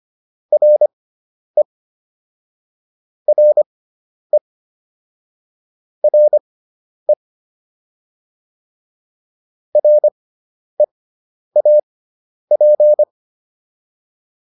Morse code, heard as a keyed tone, about 25 wpm, spaced as for 10 wpm, 600 Hz. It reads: RE RE RE REAP